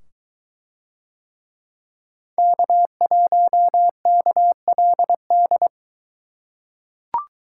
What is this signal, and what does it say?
Morse code, heard as a keyed tone, about 22 wpm, 700 Hz. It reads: K1KLD